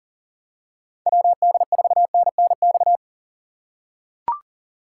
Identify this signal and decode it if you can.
Morse code, heard as a keyed tone, about 40 wpm, 700 Hz. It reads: WD4NNX
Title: USA Call Signs - 4x Character Spacing - 40wpm